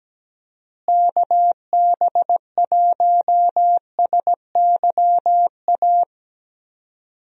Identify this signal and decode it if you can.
Morse code, heard as a keyed tone, about 17 wpm, 700 Hz. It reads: KB1SYA
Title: USA Call Signs - 6 Characters - Repeated 2x - 17wpm